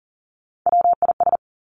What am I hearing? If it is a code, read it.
Morse code, heard as a keyed tone, about 42 wpm, 700 Hz. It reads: WIS
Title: Three Challenging Letter-Number Combinations 40wpm